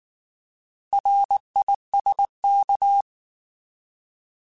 Morse code, heard as a keyed tone, about 19 wpm, 800 Hz. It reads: RISK
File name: Alphabetical / Alphabetical mp3s / Rs @18